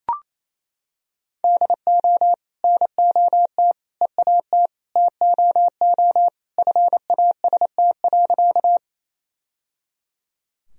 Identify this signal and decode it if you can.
Morse code, heard as a keyed tone, about 28 wpm, 700 Hz. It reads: DO NOT EAT TOO FAST.
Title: Sentences from Top 1000 Words - Review Part 2 - 28wpm